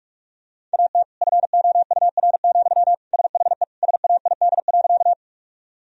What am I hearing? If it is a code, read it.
Morse code, heard as a keyed tone, about 43 wpm, 700 Hz. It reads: AT FOUR, SHE SAID.
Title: Sentences from Top 1000 Words - Review Part 1 - 45wpm